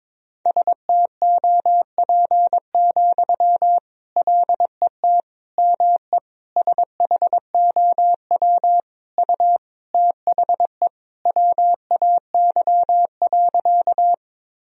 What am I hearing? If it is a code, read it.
Morse code, heard as a keyed tone, about 22 wpm, 700 Hz. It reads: STOP, LET ME SHOW U THE WAY.
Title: Sentences from Top 300 Words and Abbreviations 22wpm